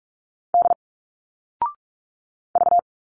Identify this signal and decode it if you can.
Morse code, heard as a keyed tone, about 46 wpm, 700 Hz. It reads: D V
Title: Single Letters 45wpm